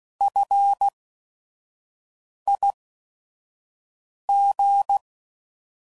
Morse code, heard as a keyed tone, about 16 wpm, spaced as for 5 wpm, 800 Hz. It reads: FIG